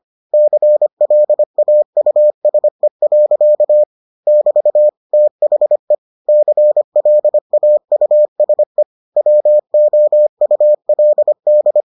Morse code, heard as a keyed tone, about 25 wpm, 600 Hz. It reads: CLAUSE. <BT> THE CLAUSE WOULD